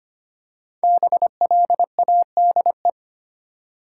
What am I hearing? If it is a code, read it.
Morse code, heard as a keyed tone, about 25 wpm, 700 Hz. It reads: BLADE
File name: 5 Letter Words - First 3 Letters Given - 25wpm